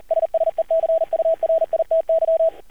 Morse code, heard as a keyed tone, about 38 wpm, 650 Hz, with radio noise. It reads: SSECURITY
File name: archived / CRSnet